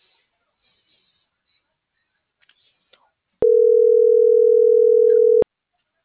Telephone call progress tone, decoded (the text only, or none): ringback tone